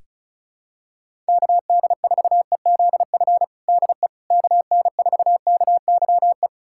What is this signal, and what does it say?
Morse code, heard as a keyed tone, about 34 wpm, 700 Hz. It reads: KD4EZF DE KN4KYE
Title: USA Callsign DE Callsign 35wpm